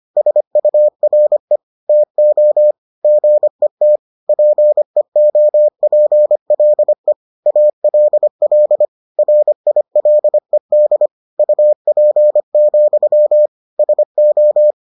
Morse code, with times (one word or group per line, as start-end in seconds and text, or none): 0.2-1.6 SURE
1.9-2.7 TO
3.0-4.0 GET
4.3-7.1 PEOPLE
7.5-8.9 ALL
9.2-11.1 RILED
11.4-13.5 UP,
13.8-14.7 SO